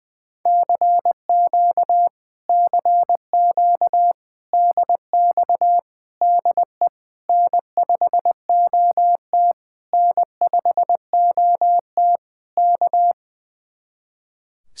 Morse code, heard as a keyed tone, about 20 wpm, 700 Hz. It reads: CQ CQ DX DE N5OT N5OT K
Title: Speed-Racing Common QSO Phrases - 20wpm